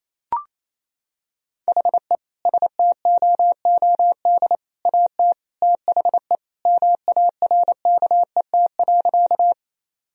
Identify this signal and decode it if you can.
Morse code, heard as a keyed tone, about 28 wpm, 700 Hz. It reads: HE STOOD AT THE MARKET.
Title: Sentences from Top 1000 Words - Review Part 3 - 28wpm